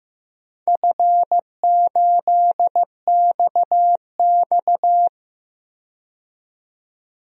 Morse code, heard as a keyed tone, about 15 wpm, 700 Hz. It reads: F8XX
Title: DXCC-100 Call Signs - 4 Characters - 15wpm